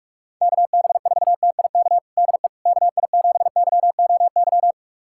Morse code, heard as a keyed tone, about 41 wpm, 700 Hz. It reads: KB4TIK DE KI7YOY
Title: USA Callsign DE Callsign - 6 Characters - 45wpm